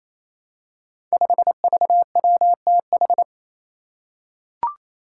Morse code, heard as a keyed tone, about 28 wpm, 700 Hz. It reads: 5VWTH
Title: Five Challenging Letter-Number Combinations 28wpm